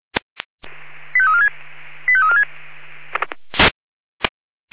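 Five-tone selective call (ZVEI), heard as quick runs of five tones, tones of about 70 ms. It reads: 85346 85346
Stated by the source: ZVEI five-tone selective call received on SDR.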